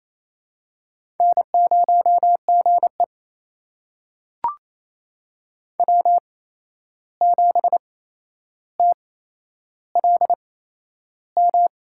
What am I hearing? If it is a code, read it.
Morse code, heard as a keyed tone, about 28 wpm, spaced as for 11 wpm, 700 Hz. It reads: N0GE W7TLM
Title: USA Call Signs - 8x Character Spacing - 28wpm